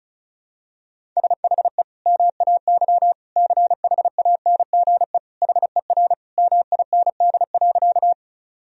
Morse code, heard as a keyed tone, about 36 wpm, 700 Hz. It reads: SHE MAY CHANGE HER MIND.
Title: Sentences from Top 1000 Words - Review Part 1 - 35wpm